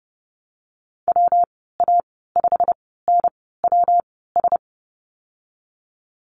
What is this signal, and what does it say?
Morse code, heard as a keyed tone, about 30 wpm, spaced as for 17 wpm, 700 Hz. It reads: WA5NWS
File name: Lesson 19g - Letter w - Call Signs 30wpm